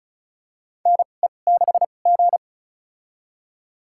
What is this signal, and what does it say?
Morse code, heard as a keyed tone, about 36 wpm, spaced as for 25 wpm, 690 Hz. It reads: NE6G
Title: USA Call Signs - 4 Characters - 2x Character Spacing - 35wpm